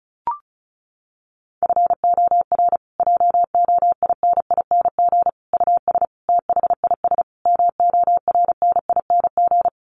Morse code, heard as a keyed tone, about 34 wpm, 700 Hz. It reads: FOR JOINING US THIS MORNING
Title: Sets of 5 Words Encore 35wpm